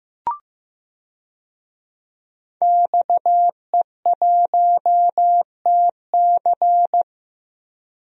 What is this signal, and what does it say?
Morse code, heard as a keyed tone, about 15 wpm, 700 Hz. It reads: XE1TC
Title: DXCC-100 Call Signs - 5 Characters - Repeated 3x - 15wpm